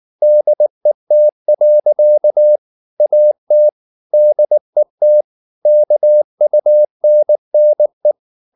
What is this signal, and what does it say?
Morse code, heard as a keyed tone, about 19 wpm, 600 Hz. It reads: DET. AT DET KUNNE